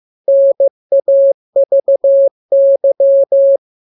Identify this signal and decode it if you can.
Morse code, heard as a keyed tone, about 15 wpm, 550 Hz. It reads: NAVY
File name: jokes / joke 2 15 0000